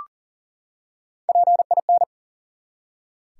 Morse code, heard as a keyed tone, about 40 wpm, 700 Hz. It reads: PIN
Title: Three Letter Words - 40wpm